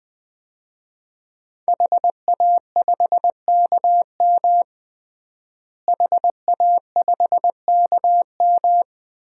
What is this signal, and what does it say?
Morse code, heard as a keyed tone, about 20 wpm, 700 Hz. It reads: HA5KM HA5KM
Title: DXCC-100 Call Signs - 5 Characters - Repeated 2x - 20wpm